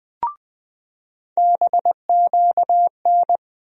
Morse code, heard as a keyed tone, about 20 wpm, 700 Hz. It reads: BQN
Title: USA Call Sign Postfixes - 3 Letters 20wpm